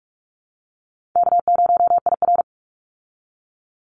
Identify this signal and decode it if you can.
Morse code, heard as a keyed tone, about 44 wpm, 700 Hz. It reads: K0IR
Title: Easy Call Signs Repeated 2x - 45wpm